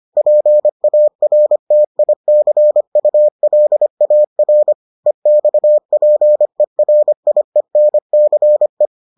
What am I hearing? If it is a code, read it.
Morse code, heard as a keyed tone, about 25 wpm, 600 Hz. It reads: PARTICULAR EXPERIENCE